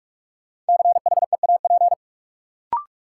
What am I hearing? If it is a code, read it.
Morse code, heard as a keyed tone, about 46 wpm, 700 Hz. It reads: CHEAP
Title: Sentences from Top 1200 Words - New Word Drill - 45wpm